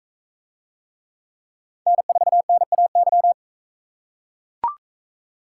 Morse code, heard as a keyed tone, about 40 wpm, 700 Hz. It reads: N4NAY